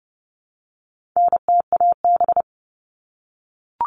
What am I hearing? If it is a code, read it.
Morse code, heard as a keyed tone, about 30 wpm, 700 Hz. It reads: NTAB
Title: All Cut Numbers - 4-Digit 30wpm